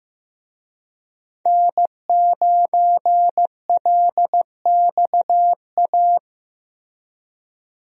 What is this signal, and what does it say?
Morse code, heard as a keyed tone, about 15 wpm, 700 Hz. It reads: N9LXA
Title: USA Call Signs - 5 Characters - 15wpm